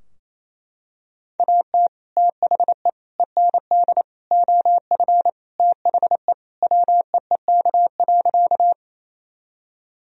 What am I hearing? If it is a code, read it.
Morse code, heard as a keyed tone, about 27 wpm, 700 Hz. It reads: AT THE END OF THE WEEK.
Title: Sentences from Top 1000 Words - Review Part 2 - 28wpm